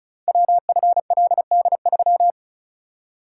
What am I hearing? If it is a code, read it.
Morse code, heard as a keyed tone, about 36 wpm, 700 Hz. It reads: WFLD3
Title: Code Groups - 5 Letter:Numbers - 35wpm